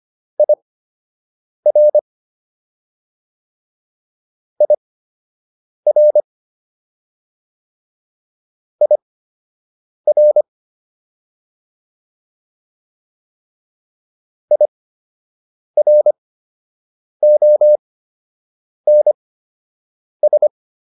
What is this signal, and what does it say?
Morse code, heard as a keyed tone, about 25 wpm, spaced as for 7 wpm, 600 Hz. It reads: IR IR IR IRONS